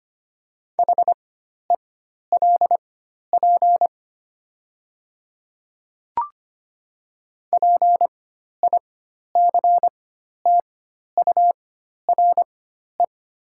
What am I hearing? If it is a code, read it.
Morse code, heard as a keyed tone, about 25 wpm, spaced as for 12 wpm, 700 Hz. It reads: HELP PICTURE
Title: 4x Character Spacing - Top 200 Words - 25wpm